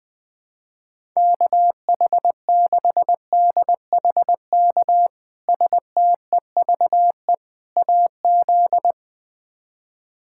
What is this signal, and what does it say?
Morse code, heard as a keyed tone, about 20 wpm, 700 Hz. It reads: KH6DHK STEVE AZ
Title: K1USN SST QSO Practice - 20wpm